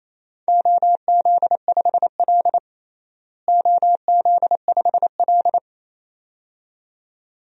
Morse code, heard as a keyed tone, about 28 wpm, 700 Hz. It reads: OZ5L OZ5L
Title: DXCC-100 Call Signs - 4 Characters - Repeated 2x - 28wpm